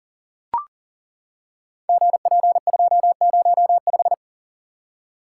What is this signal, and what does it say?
Morse code, heard as a keyed tone, about 41 wpm, 700 Hz. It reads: GP205
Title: Code Groups - 5 Letter:Numbers - 40wpm